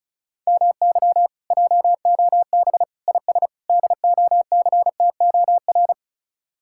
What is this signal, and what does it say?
Morse code, heard as a keyed tone, about 43 wpm, 700 Hz. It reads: MY JOB IS DOCTOR
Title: Speed-Racing Common QSO Phrases - 35wpm